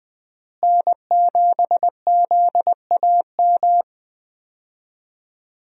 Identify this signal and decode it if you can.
Morse code, heard as a keyed tone, about 20 wpm, 700 Hz. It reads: N7ZAM
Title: USA Call Signs - 5 Characters - Repeated 3x - 20wpm